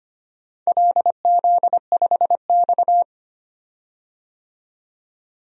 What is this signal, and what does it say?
Morse code, heard as a keyed tone, about 25 wpm, 700 Hz. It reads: LZ5X
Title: DXCC-100 Call Signs - 4 Characters - 25wpm